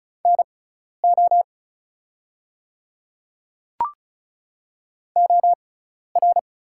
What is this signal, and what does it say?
Morse code, heard as a keyed tone, about 36 wpm, spaced as for 12 wpm, 690 Hz. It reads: NO OR
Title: 6x Character Spacing - Top 200 Words - 35wpm